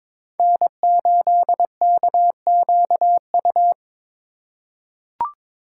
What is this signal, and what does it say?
Morse code, heard as a keyed tone, about 22 wpm, 700 Hz. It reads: N8KQU
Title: USA Call Signs - 5 Characters - 3x Character Spacing - 22wpm